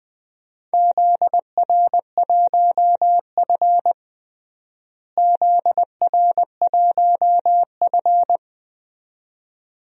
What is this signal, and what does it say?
Morse code, heard as a keyed tone, about 20 wpm, 700 Hz. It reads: ZR1F ZR1F
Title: DXCC-100 Call Signs - 4 Characters - Repeated 2x - 20wpm